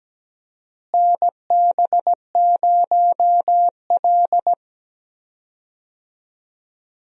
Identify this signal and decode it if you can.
Morse code, heard as a keyed tone, about 17 wpm, 700 Hz. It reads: NB0L
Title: USA Call Signs - 4 Characters - Repeated 2x - 17wpm